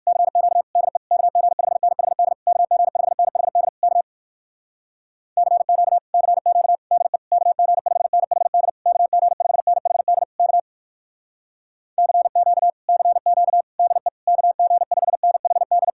Morse code, heard as a keyed tone, about 67 wpm, 700 Hz. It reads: CQ DE KG5NHD KG5NHD K CQ CQ DE KG5NHD KG5NHD K CQ CQ DE KG5NHD